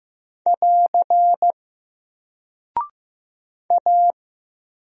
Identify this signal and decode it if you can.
Morse code, heard as a keyed tone, about 15 wpm, 700 Hz. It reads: <AR> A
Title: Single Character (Letter:Number:Punctuation:Prosign) 15wpm